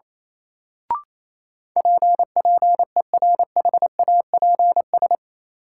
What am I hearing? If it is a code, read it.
Morse code, heard as a keyed tone, about 28 wpm, 700 Hz. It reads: PPERHAPS
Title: Sentences from Top 1100 Words - New Word Drill Alt - 28wpm